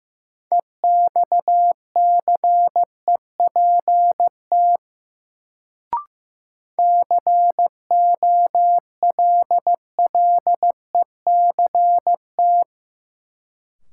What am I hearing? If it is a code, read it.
Morse code, heard as a keyed tone, about 15 wpm, 700 Hz. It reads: EXCEPT COLLECT
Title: Sentences from Top 1600 Words - New Word Drill - 15wpm